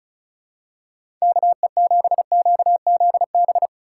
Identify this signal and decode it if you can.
Morse code, heard as a keyed tone, about 34 wpm, 700 Hz. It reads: KE7QZB
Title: USA Call Signs - 6 Characters - 2x Character Spacing - 35wpm